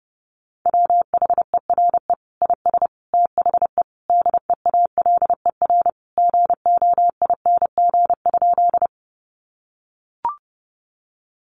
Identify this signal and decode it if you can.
Morse code, heard as a keyed tone, about 28 wpm, 700 Hz. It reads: WHERE IS THE DEALER GOING?
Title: Sentences from Top 1100 Words 30wpm